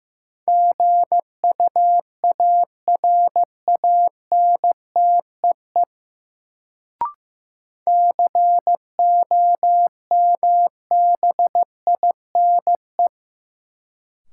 Morse code, heard as a keyed tone, about 15 wpm, 700 Hz. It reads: GUARANTEE COMBINE
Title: Sentences from Top 1600 Words - New Word Drill Alt - 15wpm